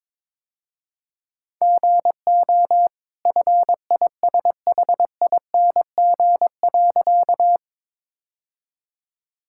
Morse code, heard as a keyed tone, about 22 wpm, 690 Hz. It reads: GO FISHING.
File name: practice / 22wpm / Easy Sentences with Hints - 22wpm